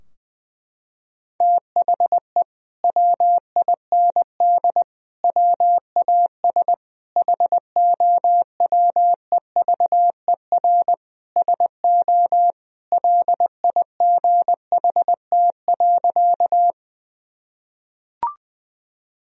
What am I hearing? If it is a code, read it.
Morse code, heard as a keyed tone, about 20 wpm, 700 Hz. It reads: THE WIND WAS HOWEVER SO LIGHT.
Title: Sentences from Top 1000 Words - Review Part 2 - 20wpm